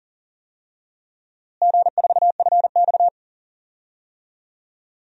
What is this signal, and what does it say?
Morse code, heard as a keyed tone, about 40 wpm, 700 Hz. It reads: G4FX
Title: DXCC-100 Call Signs - 4 Characters - Repeated 2x - 40wpm